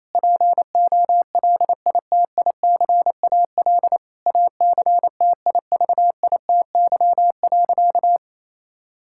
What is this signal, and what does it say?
Morse code, heard as a keyed tone, about 28 wpm, 700 Hz. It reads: POLITICAL ACTIVITY.
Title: Sentences from Top 1000 Words - Review Part 5 - 28wpm